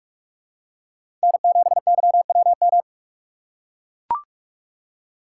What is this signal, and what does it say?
Morse code, heard as a keyed tone, about 43 wpm, 700 Hz. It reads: N7YWM
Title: USA Call Signs - 6x Character Spacing - 45wpm